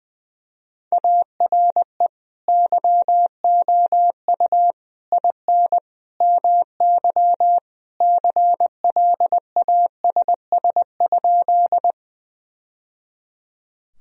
Morse code, heard as a keyed tone, about 21 wpm, 700 Hz. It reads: ARE YOU IN MY CLASS?
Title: Sentences from Top 1000 Words - Review Part 2 - 20wpm